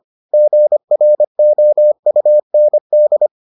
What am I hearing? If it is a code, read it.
Morse code, heard as a keyed tone, about 25 wpm, 600 Hz. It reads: GROUND